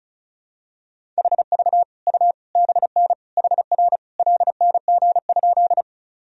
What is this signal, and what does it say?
Morse code, heard as a keyed tone, about 34 wpm, 700 Hz. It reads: HV U BN HR LNG?
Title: Sentences from Top 200 Words and Abbreviations 35wpm